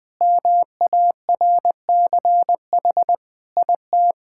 Morse code, heard as a keyed tone, about 21 wpm, 700 Hz. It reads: MARCH IT
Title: Sentences with 5-Word Phrases - 20wpm